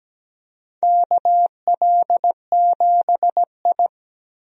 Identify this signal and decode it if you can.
Morse code, heard as a keyed tone, about 17 wpm, 700 Hz. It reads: KL7I
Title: DXCC-100 Call Signs - 4 Characters - 17wpm